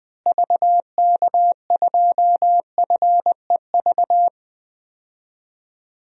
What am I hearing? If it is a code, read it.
Morse code, heard as a keyed tone, about 21 wpm, 700 Hz. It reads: VK2FEV
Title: DXCC-100 Call Signs - 6 Characters - Repeated 2x - 20wpm